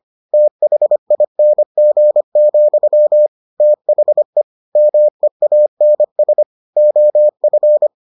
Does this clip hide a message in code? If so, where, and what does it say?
Yes: Morse code from 0.3 to 7.9 s, heard as a keyed tone, about 25 wpm, 600 Hz. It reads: THING, THE MEANS OF